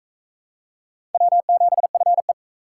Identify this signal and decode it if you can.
Morse code, heard as a keyed tone, about 38 wpm, 700 Hz. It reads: W7FE